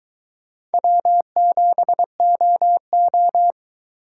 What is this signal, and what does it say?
Morse code, heard as a keyed tone, about 22 wpm, 700 Hz. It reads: W7OO